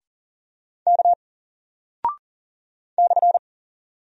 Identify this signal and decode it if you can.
Morse code, heard as a keyed tone, about 40 wpm, 700 Hz. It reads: K /
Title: Single Character (Letter:Number:Punctuation:Prosign) Alt 40wpm